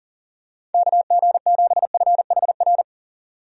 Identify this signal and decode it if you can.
Morse code, heard as a keyed tone, about 40 wpm, 700 Hz. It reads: KG7FHR
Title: USA Call Signs - 6 Characters - 40wpm